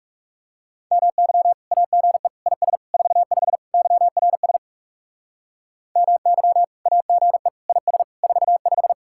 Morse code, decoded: MY AGE IS 45 YRS MY AGE IS 45

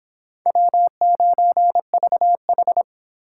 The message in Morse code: W9VH

